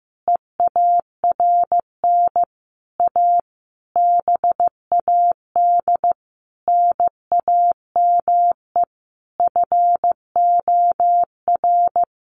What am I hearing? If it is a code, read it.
Morse code, heard as a keyed tone, about 15 wpm, 700 Hz. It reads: EARN A BAD NAME FOR
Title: Sentences from Top 1500 Words 15wpm